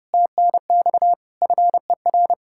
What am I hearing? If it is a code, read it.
Morse code, heard as a keyed tone, about 31 wpm, 700 Hz. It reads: TNX FER